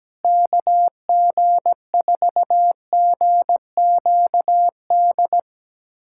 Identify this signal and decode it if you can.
Morse code, heard as a keyed tone, about 17 wpm, 700 Hz. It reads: KG4GQD
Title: USA Callsign DE Callsign - 6 Characters - 17wpm